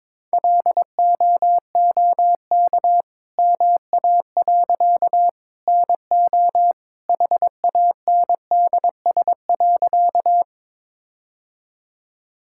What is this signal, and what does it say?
Morse code, heard as a keyed tone, about 21 wpm, 690 Hz. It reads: LOOK MA. NO HANDS.
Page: morse-code-ninja-practice-22wpm